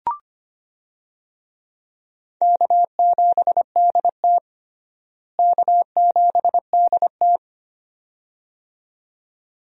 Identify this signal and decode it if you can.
Morse code, heard as a keyed tone, about 25 wpm, 700 Hz. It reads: K7DT K7DT
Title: USA Call Signs - 4 Characters - Repeated 2x - 25wpm